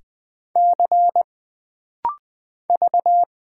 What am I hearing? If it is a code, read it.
Morse code, heard as a keyed tone, about 20 wpm, 700 Hz. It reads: C V
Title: Single Character (Letter:Number:Punctuation:Prosign) - Rapid-Fire - 20wpm